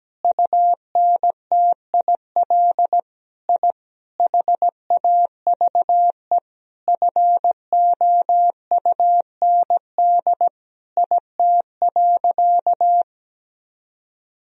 Morse code, heard as a keyed tone, about 17 wpm, 700 Hz. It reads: UNTIL I HAVE FOUND IT.
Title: Sentences from Top 1000 Words - Review Part 1 - 17wpm